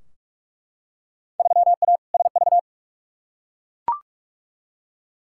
Morse code, heard as a keyed tone, about 42 wpm, 700 Hz. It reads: 3A SV
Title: Field Day Exchanges - 45wpm